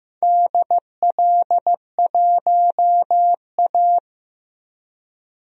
Morse code, heard as a keyed tone, about 16 wpm, 700 Hz. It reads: DL1A
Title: DXCC-100 Call Signs - 4 Characters - Repeated 3x - 15wpm